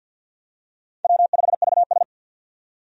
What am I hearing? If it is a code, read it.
Morse code, heard as a keyed tone, about 53 wpm, 700 Hz. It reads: W5VS
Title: USA Call Signs - 4 Characters - 50wpm